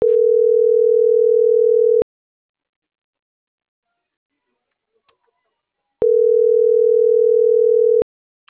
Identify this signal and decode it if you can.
Ringback tone (North America), 440 + 480 Hz.